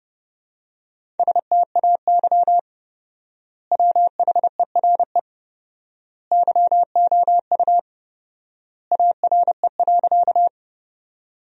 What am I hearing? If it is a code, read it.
Morse code, heard as a keyed tone, about 32 wpm, 700 Hz. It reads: STAY WHERE YOU ARE.
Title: 4x Word-Spacing - Sentences from Top 500 Words - 30wpm